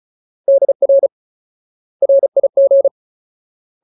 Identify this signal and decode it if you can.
Morse code, heard as a keyed tone, about 34 wpm, 550 Hz. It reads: DR RIG
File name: Ultimate Word List / 35WPM Ultimate